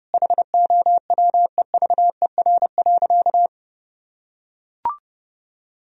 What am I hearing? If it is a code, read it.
Morse code, heard as a keyed tone, about 30 wpm, 700 Hz. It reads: HOWEVER.
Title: Sentences from Top 1000 Words - Review Part 1 - 30wpm